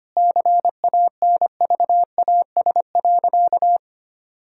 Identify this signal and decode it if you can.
Morse code, heard as a keyed tone, about 25 wpm, 700 Hz. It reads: CANVAS.
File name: practice / 25wpm / Difficult Sentences with Hints - 25wpm